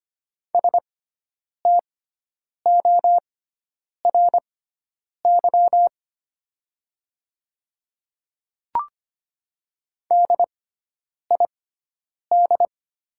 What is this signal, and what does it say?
Morse code, heard as a keyed tone, about 25 wpm, spaced as for 9 wpm, 700 Hz. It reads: STORY DID
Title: 6x Character Spacing - Top 300 Words - 25wpm